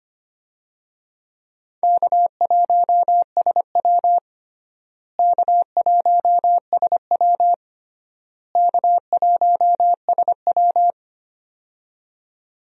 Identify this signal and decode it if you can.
Morse code, heard as a keyed tone, about 25 wpm, 700 Hz. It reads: K1SW K1SW K1SW
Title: USA Call Signs - 4 Characters - Repeated 3x - 25wpm